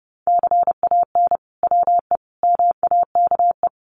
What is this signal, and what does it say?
Morse code, heard as a keyed tone, about 30 wpm, 700 Hz. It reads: CAN WE MAKE
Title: Sentences from Top 1500 Words 30wpm